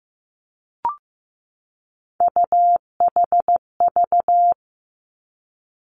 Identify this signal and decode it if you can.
Morse code, heard as a keyed tone, about 15 wpm, 700 Hz. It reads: UHV
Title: Three Challenging Letter-Number Combinations 15wpm